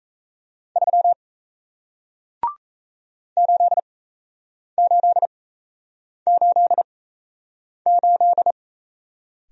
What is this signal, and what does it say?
Morse code, heard as a keyed tone, about 37 wpm, spaced as for 8 wpm, 700 Hz. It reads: 3 8888